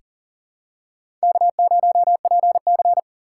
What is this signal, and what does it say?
Morse code, heard as a keyed tone, about 40 wpm, 700 Hz. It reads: K0PC
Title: K1USN SST QSO Practice - 40wpm